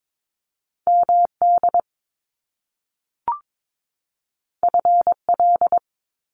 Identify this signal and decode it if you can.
Morse code, heard as a keyed tone, about 22 wpm, 700 Hz. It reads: MD FL
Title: US States 22wpm